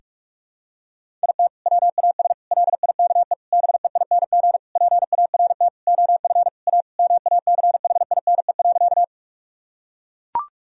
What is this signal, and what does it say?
Morse code, heard as a keyed tone, about 42 wpm, 700 Hz. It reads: IT WAS LIKE BEING PART OF A MACHINE.
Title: Sentences from Top 1000 Words - Review Part 3 - 45wpm